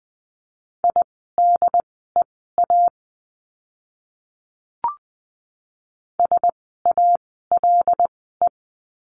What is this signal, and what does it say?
Morse code, heard as a keyed tone, about 20 wpm, spaced as for 15 wpm, 700 Hz. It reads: IDEA SALE
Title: 075 Lesson 13f - Letter L - Words 20wpm